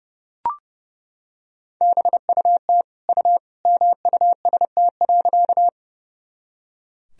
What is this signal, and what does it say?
Morse code, heard as a keyed tone, about 30 wpm, 700 Hz. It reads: BUT U MUST.
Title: Sentences from Top 200 Words and Abbreviations 30wpm